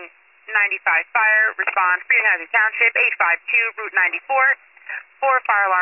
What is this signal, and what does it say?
Emergency Alert System (SAME) tones (header bursts), heard as harsh buzzing bursts, with speech. It is heard from 0.5 to 4.8 s.